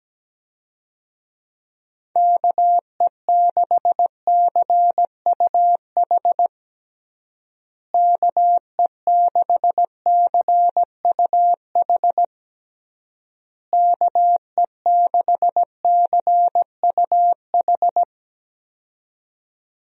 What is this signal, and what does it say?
Morse code, heard as a keyed tone, about 17 wpm, 700 Hz. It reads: KE6CUH KE6CUH KE6CUH
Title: USA Call Signs - 6 Characters - Repeated 3x - 17wpm